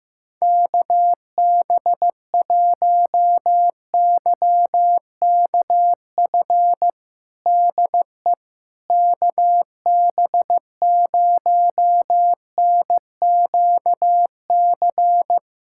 Morse code, heard as a keyed tone, about 16 wpm, 700 Hz. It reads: KB1YKF DE KB0NQC
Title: USA Callsign DE Callsign - 6 Characters - 15wpm